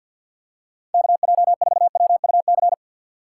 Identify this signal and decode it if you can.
Morse code, heard as a keyed tone, about 53 wpm, 700 Hz. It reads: KJ4WUC
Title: USA Call Signs - 6 Characters - 6x Character Spacing - 50wpm